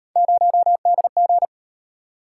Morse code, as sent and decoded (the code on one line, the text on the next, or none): ----- -.. --.
0DG